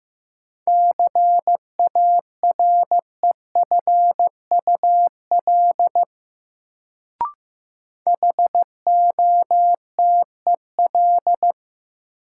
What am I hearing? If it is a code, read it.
Morse code, heard as a keyed tone, about 15 wpm, 700 Hz. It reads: CAREFUL HOTEL
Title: Sentences from Top 1600 Words - New Word Drill - 15wpm